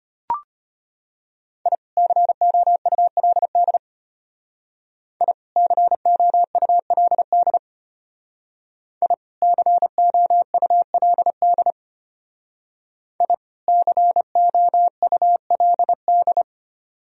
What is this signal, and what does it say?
Morse code, heard as a keyed tone, about 30 wpm, 700 Hz. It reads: I COULD I COULD I COULD I COULD
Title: Speed-Racing - Sets of 2 Words 25wpm